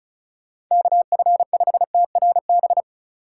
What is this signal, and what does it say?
Morse code, heard as a keyed tone, about 34 wpm, 700 Hz. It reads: KF5TRB
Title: USA Call Signs - 6 Characters - 3x Character Spacing - 35wpm